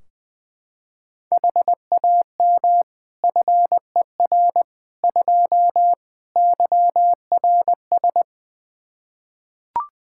Morse code, heard as a keyed tone, about 21 wpm, 700 Hz. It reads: HAM FER 2 YRS
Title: Practice QSO - Exchange 3 - More Info - 20wpm